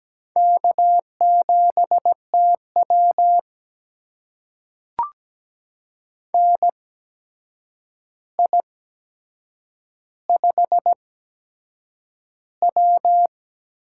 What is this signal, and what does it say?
Morse code, heard as a keyed tone, about 17 wpm, spaced as for 7 wpm, 700 Hz. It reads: K7TW NI5W
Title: USA Call Signs - 4 Characters - 8x Character Spacing - 17wpm